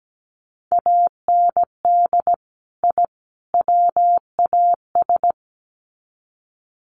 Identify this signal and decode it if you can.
Morse code, heard as a keyed tone, about 17 wpm, 700 Hz. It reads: AND I WAS
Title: Sets of 3 Words 17wpm